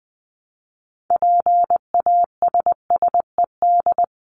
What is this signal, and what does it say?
Morse code, heard as a keyed tone, about 20 wpm, 700 Hz. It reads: PASSED
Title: ED Suffix - Top 100 Words with Base Words 20wpm